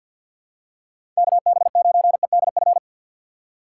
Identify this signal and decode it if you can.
Morse code, heard as a keyed tone, about 51 wpm, 700 Hz. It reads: KB9EDF
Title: USA Call Signs - 6 Characters - 6x Character Spacing - 50wpm